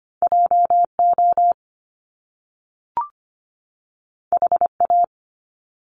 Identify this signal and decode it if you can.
Morse code, heard as a keyed tone, about 25 wpm, 700 Hz. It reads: JO HA
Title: Top Two Letter Combinations in Words 25wpm